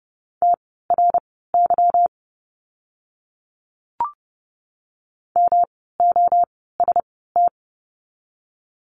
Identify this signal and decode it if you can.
Morse code, heard as a keyed tone, about 30 wpm, spaced as for 17 wpm, 700 Hz. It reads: TRY MOST
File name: course / 30:10wpm / 230 Lesson 37f - Letter x - Words 30wpm